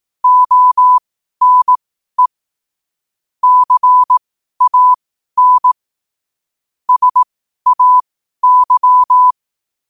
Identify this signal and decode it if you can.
Morse code, heard as a keyed tone, about 18 wpm, spaced as for 12 wpm, 1000 Hz. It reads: ONE CAN SAY
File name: qotdfiles / 12wpm